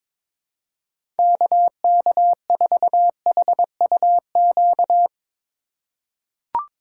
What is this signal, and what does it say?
Morse code, heard as a keyed tone, about 22 wpm, 700 Hz. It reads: KK4HUQ
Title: USA Call Signs - 6 Characters - 4x Character Spacing - 22wpm